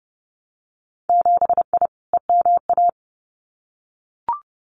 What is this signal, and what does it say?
Morse code, heard as a keyed tone, about 30 wpm, 700 Hz. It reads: 7I EMA
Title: Winter Field Day Exchanges Encore 30wpm